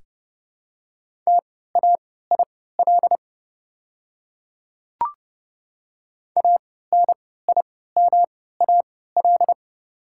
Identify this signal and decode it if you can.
Morse code, heard as a keyed tone, about 30 wpm, spaced as for 17 wpm, 700 Hz. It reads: TAIL ANIMAL
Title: 3x Character Spacing - Top 500 Words - 30wpm